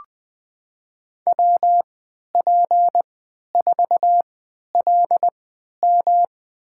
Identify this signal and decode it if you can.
Morse code, heard as a keyed tone, about 21 wpm, spaced as for 12 wpm, 700 Hz. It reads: WP4LM
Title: USA Call Signs - 5 Characters - 3x Character Spacing - 20wpm